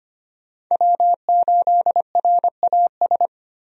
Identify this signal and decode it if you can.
Morse code, heard as a keyed tone, about 25 wpm, 690 Hz. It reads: W8RAS
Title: USA Call Signs - 5 Characters - 25wpm